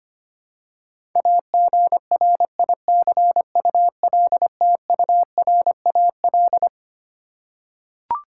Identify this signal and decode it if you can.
Morse code, heard as a keyed tone, about 25 wpm, 700 Hz. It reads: AGRICULTURAL
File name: Twelve Letter Words - 25wpm